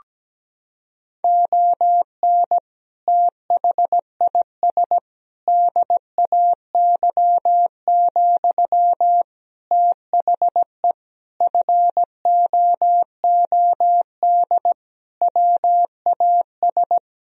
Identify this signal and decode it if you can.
Morse code, heard as a keyed tone, about 17 wpm, 700 Hz. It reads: ON THIS DAY, THE FOOD WAS
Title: Sentences from Top 1000 Words - Review Part 2 - 17wpm